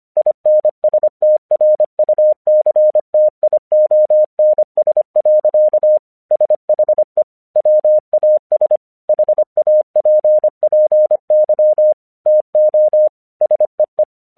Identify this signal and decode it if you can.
Morse code, heard as a keyed tone, about 25 wpm, 610 Hz. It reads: INSTRUCTIONS. SHE WAS HAPPY TO SEE
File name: 1 English 25wpm / AQC23